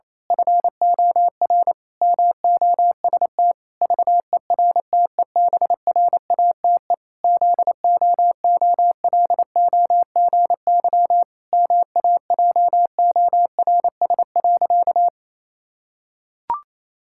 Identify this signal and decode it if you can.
Morse code, heard as a keyed tone, about 28 wpm, 700 Hz. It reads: FOR MOST VERTEBRATE ZOOLOGY MAJORS.
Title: Pangram Challenge v2 - 28wpm